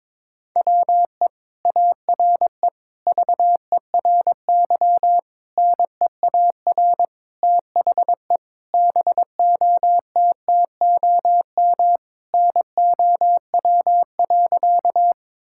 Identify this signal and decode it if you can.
Morse code, heard as a keyed tone, about 22 wpm, 700 Hz. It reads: WE ARE VERY NEAR THE BOTTOM NOW.
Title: Sentences from Top 1000 Words - Review Part 6 - 22wpm